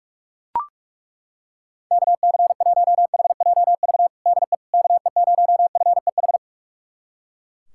Morse code, heard as a keyed tone, about 41 wpm, 700 Hz. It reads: KC1HJV DE KE0FEH
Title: USA Callsign DE Callsign 45wpm